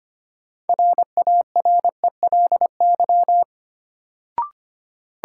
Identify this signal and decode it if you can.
Morse code, heard as a keyed tone, about 25 wpm, 700 Hz. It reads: RARELY
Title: Sentences from Top 1400 Words - New Word Drill Alt - 25wpm